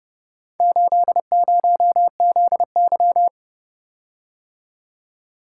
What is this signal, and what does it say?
Morse code, heard as a keyed tone, about 30 wpm, 700 Hz. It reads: 80ZY